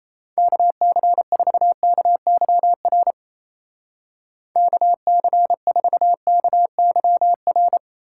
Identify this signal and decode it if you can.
Morse code, heard as a keyed tone, about 34 wpm, 700 Hz. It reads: KC4KYR KC4KYR
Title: Speed Racing - USA 2x3 Call Signs 28wpm